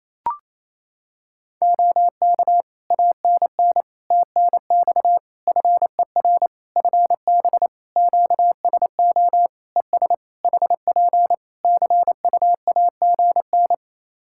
Morse code, heard as a keyed tone, about 27 wpm, 700 Hz. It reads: OK ANN TNX FER FB QSO ES HP CUAGN
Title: Practice QSO - Exchange 4 - Ending - 28wpm